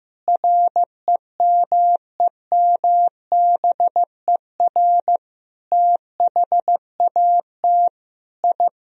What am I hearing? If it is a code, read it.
Morse code, heard as a keyed tone, about 15 wpm, 700 Hz. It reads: REMEMBER THAT I
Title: Sentences from Top 1000 Words - Review Part 1 - 15wpm